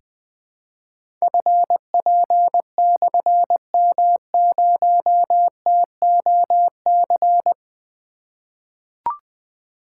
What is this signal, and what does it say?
Morse code, heard as a keyed tone, about 20 wpm, 700 Hz. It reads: FP/M0TOC
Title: Difficult Call Signs 20wpm